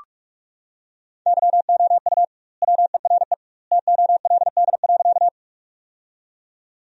Morse code, heard as a keyed tone, about 46 wpm, 700 Hz. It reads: YOU WERE TOLD.
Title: 2x Word-Spacing - Sentences from Top 400 Words - 45wpm